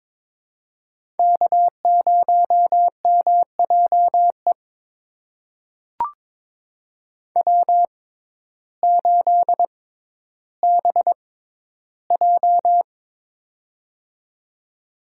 Morse code, heard as a keyed tone, about 22 wpm, spaced as for 12 wpm, 700 Hz. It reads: K0MJE W8BJ
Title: USA Call Signs - 6x Character Spacing - 22wpm